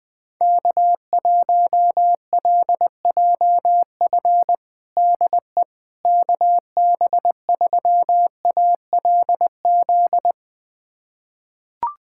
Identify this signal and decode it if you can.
Morse code, heard as a keyed tone, about 21 wpm, 700 Hz. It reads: K1LJF DE KB3ALZ
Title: USA Callsign DE Callsign 20wpm